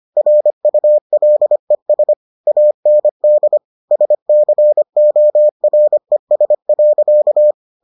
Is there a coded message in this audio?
Yes: Morse code from 0.2 to 7.5 s, heard as a keyed tone, about 25 wpm, 600 Hz. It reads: RULES AND SCORES.